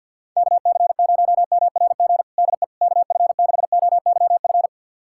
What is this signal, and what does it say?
Morse code, heard as a keyed tone, about 52 wpm, 700 Hz. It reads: KC0MRG DE KF6OYF
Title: USA Callsign DE Callsign - 6 Characters - 50wpm